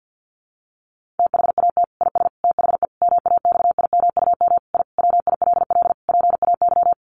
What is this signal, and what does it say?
Morse code, heard as a keyed tone, about 50 wpm, 700 Hz. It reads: THAT IS THE MAXIMUM I WILL PAY
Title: Sentences from Top 1300 Words 50wpm